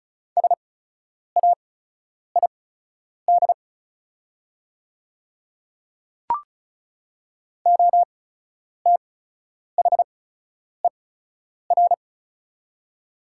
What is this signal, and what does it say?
Morse code, heard as a keyed tone, about 36 wpm, spaced as for 10 wpm, 700 Hz. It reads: SAID OTHER
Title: 8x Character Spacing - Top 100 Words - 35wpm